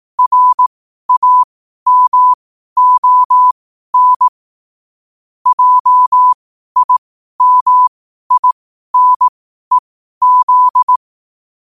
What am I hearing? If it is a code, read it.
Morse code, heard as a keyed tone, about 18 wpm, spaced as for 12 wpm, 1000 Hz. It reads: RAMON JIMINEZ
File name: qotdfiles / 12wpm